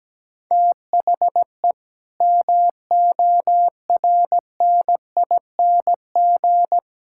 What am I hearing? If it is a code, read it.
Morse code, heard as a keyed tone, about 17 wpm, 700 Hz. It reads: THE MORNING
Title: Sentences from Top 1600 Words 17wpm